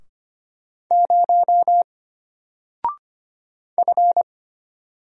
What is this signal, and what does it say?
Morse code, heard as a keyed tone, about 25 wpm, 700 Hz. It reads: 0 F
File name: Single-Letter-Number-25wpm